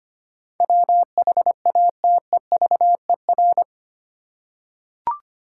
Morse code, heard as a keyed tone, about 25 wpm, 700 Hz. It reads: WHATEVER
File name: 8 Letter Words - First 3 Letters Given - 25wpm